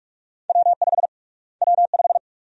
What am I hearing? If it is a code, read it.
Morse code, heard as a keyed tone, about 44 wpm, 700 Hz. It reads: W5 W5
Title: USA Call Sign Prefixes - Repeated 2x - 45wpm